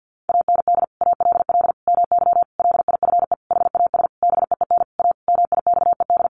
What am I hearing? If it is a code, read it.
Morse code, heard as a keyed tone, about 50 wpm, 700 Hz. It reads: AND ALL MY LIFE HAS BEEN A MIXED
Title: Sentences from Top 1500 Words 50wpm